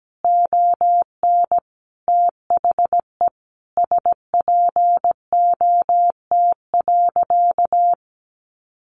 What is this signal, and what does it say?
Morse code, heard as a keyed tone, about 17 wpm, 700 Hz. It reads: ON THE SPOT.
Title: Sentences from Top 1100 Words 17wpm